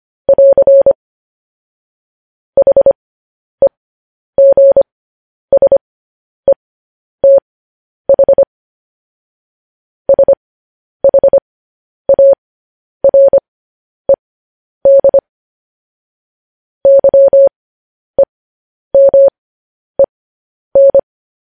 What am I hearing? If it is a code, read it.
Morse code, heard as a keyed tone, about 25 wpm, spaced as for 10 wpm, 550 Hz. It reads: <AR> HEGSETH SHARED YEMEN